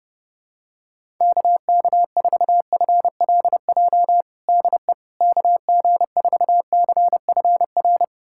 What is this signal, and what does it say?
Morse code, heard as a keyed tone, about 30 wpm, 700 Hz. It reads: KK4FLJ DE KG4CFR
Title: USA Callsign DE Callsign - 6 Characters - 30wpm